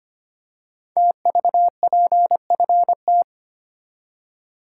Morse code, heard as a keyed tone, about 25 wpm, 700 Hz. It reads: TVPFT
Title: Code Groups - 5 Letters - 25wpm